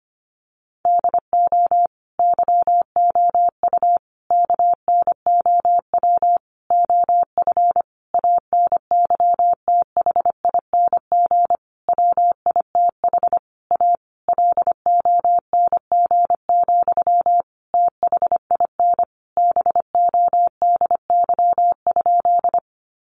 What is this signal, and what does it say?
Morse code, heard as a keyed tone, about 25 wpm, 700 Hz. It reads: DO YOU KNOW OF ANYTHING WITH A LONG, THIN BODY?
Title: Sentences from Top 1200 Words 25wpm